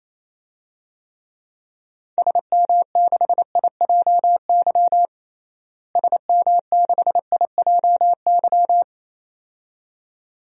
Morse code, heard as a keyed tone, about 28 wpm, 700 Hz. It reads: SM6IJY SM6IJY
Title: DXCC-100 Call Signs - 6 Characters - Repeated 2x - 28wpm